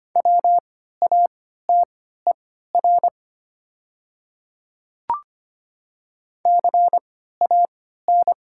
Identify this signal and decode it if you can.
Morse code, heard as a keyed tone, about 25 wpm, spaced as for 14 wpm, 700 Hz. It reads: WATER CAN
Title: 3x Character Spacing - Top 300 Words - 25wpm